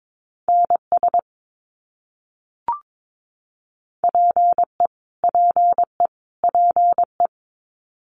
Morse code, heard as a keyed tone, about 22 wpm, 700 Hz. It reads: NS PE PE PE
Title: Canadian Provinces and Territories - Repeated 3x - 22wpm